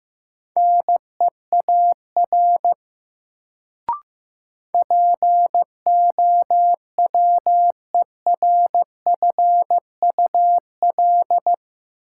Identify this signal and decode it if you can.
Morse code, heard as a keyed tone, about 15 wpm, 700 Hz. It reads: NEAR POWERFUL
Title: Sets of 5 Words Encore - New Word Drill - 15wpm